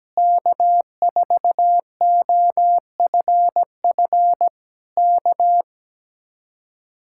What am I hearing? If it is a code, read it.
Morse code, heard as a keyed tone, about 17 wpm, 700 Hz. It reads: K4OFF K
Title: Speed-Racing Common QSO Phrases - 17wpm